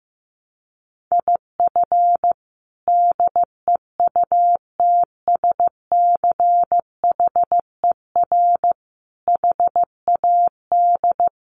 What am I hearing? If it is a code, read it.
Morse code, heard as a keyed tone, about 15 wpm, 700 Hz. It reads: IF DEUTSCHER HAD
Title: A Sound Of Thunder 15wpm